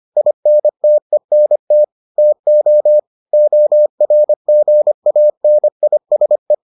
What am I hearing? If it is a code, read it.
Morse code, heard as a keyed tone, about 25 wpm, 600 Hz. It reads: INTENT TO ORGANISE